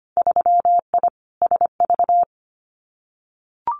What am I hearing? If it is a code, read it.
Morse code, heard as a keyed tone, about 25 wpm, 700 Hz. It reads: 3I SV
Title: Winter Field Day Exchanges Encore 25wpm